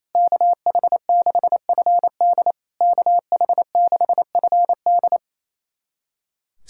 Morse code, heard as a keyed tone, about 28 wpm, 700 Hz. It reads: KH6FD KH6FD
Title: SOTA QSO Practice 28wpm